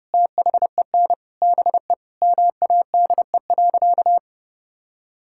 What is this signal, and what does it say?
Morse code, heard as a keyed tone, about 30 wpm, 700 Hz. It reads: THEN BE MADE.
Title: Sentences from Top 1600 Words 30wpm